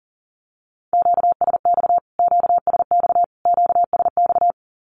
Morse code, heard as a keyed tone, about 40 wpm, 700 Hz. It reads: QSX QSX QSX
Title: Q-Codes Repeated 3x 40wpm